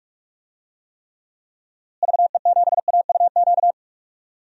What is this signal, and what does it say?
Morse code, heard as a keyed tone, about 43 wpm, 700 Hz. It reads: VE7AUQ
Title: DXCC-100 Call Signs - 6 Characters - Repeated 3x - 45wpm